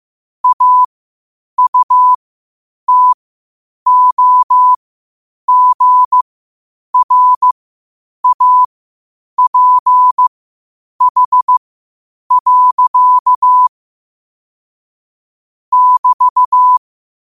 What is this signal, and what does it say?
Morse code, heard as a keyed tone, about 15 wpm, spaced as for 8 wpm, 1000 Hz. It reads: AUTOGRAPH. <BT>
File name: qotdfiles / 8wpm